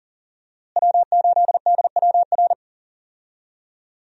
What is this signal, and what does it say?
Morse code, heard as a keyed tone, about 40 wpm, 700 Hz. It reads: W8DWR
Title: USA Call Signs - 5 Characters - Repeated 3x - 40wpm